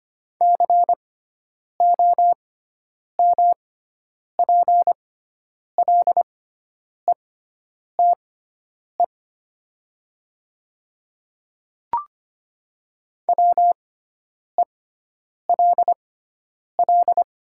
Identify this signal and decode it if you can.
Morse code, heard as a keyed tone, about 25 wpm, spaced as for 9 wpm, 700 Hz. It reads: COMPLETE WELL
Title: 6x Character Spacing - Top 500 Words - 25wpm